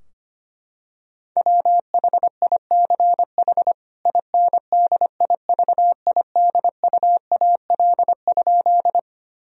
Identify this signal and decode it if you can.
Morse code, heard as a keyed tone, about 25 wpm, 700 Hz. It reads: WHICH INDIVIDUAL?
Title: Sentences from Top 1000 Words - Review Part 4 - 25wpm